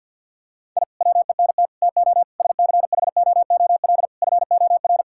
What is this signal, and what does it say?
Morse code, heard as a keyed tone, about 50 wpm, 700 Hz. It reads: I WENT TO SCHOOL FOR
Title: Sentences from Top 1000 Words - Review Part 6 - 50wpm